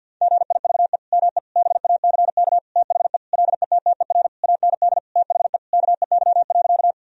Morse code, heard as a keyed tone, about 50 wpm, 700 Hz. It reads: GIVE ME BACK THE LETTER AND THE KEY.
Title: Sentences from Top 1000 Words - Review Part 4 - 50wpm